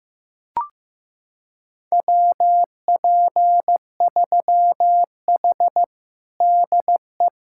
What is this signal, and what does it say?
Morse code, heard as a keyed tone, about 15 wpm, 700 Hz. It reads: WP3H DE
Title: USA Callsign DE Callsign 15wpm